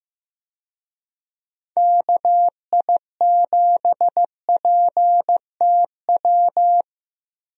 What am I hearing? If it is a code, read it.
Morse code, heard as a keyed tone, about 15 wpm, 700 Hz. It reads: KI7PTW
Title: USA Call Signs - 6 Characters - Repeated 2x - 15wpm